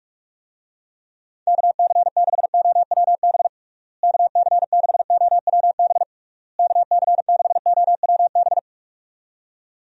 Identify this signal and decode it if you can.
Morse code, heard as a keyed tone, about 46 wpm, 700 Hz. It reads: KC6OWB KC6OWB KC6OWB
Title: USA Call Signs - 6 Characters - Repeated 3x - 45wpm